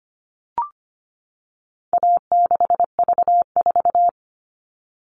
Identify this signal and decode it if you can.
Morse code, heard as a keyed tone, about 25 wpm, 700 Hz. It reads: A6V4
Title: All Cut Numbers - 4-Digit 25wpm